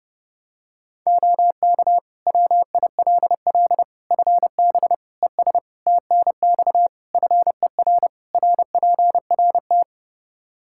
Morse code, heard as a keyed tone, about 32 wpm, 700 Hz. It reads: OK WILL FB ES TNX FER RPRT
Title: Practice QSO - Exchange 2 - Info - 30wpm